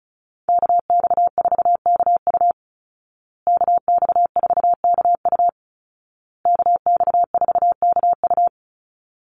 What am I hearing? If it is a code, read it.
Morse code, heard as a keyed tone, about 34 wpm, 700 Hz. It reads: KX4KU KX4KU KX4KU
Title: Call Signs Repeated 2x - 35wpm